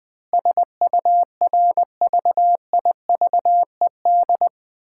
Morse code, heard as a keyed tone, about 20 wpm, 700 Hz. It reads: SURVIVED
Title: Sentences with 7-Word Phrases - 20wpm